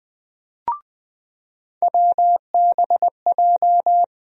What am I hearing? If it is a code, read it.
Morse code, heard as a keyed tone, about 20 wpm, 700 Hz. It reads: WBJ
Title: USA Call Sign Postfixes - 3 Letters 20wpm